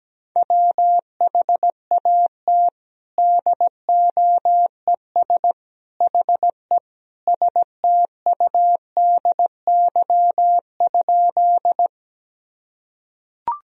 Morse code, heard as a keyed tone, about 17 wpm, 700 Hz. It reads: WHAT DOES HE STUDY?
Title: Sentences from Top 1000 Words - Review Part 1 - 17wpm